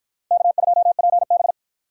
Morse code, heard as a keyed tone, about 52 wpm, 700 Hz. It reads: K2PB